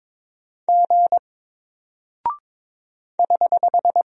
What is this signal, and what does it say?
Morse code, heard as a keyed tone, about 22 wpm, 700 Hz. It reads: G <HH>
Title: Single Character (Letter:Number:Punctuation:Prosign) Alt 22wpm